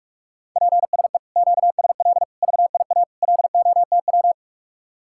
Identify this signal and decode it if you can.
Morse code, heard as a keyed tone, about 50 wpm, 700 Hz. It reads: PSE QSL VIA LOTW